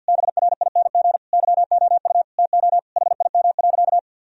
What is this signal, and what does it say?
Morse code, heard as a keyed tone, about 50 wpm, 700 Hz. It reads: BRING YOU TO HIM.
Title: Sentences from Top 1000 Words - Review Part 3 - 50wpm